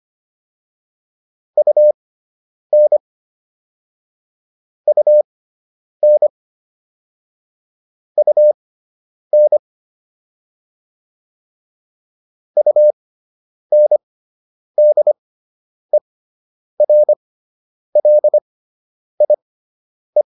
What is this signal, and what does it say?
Morse code, heard as a keyed tone, about 25 wpm, spaced as for 9 wpm, 600 Hz. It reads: UN UN UN UNDERLIE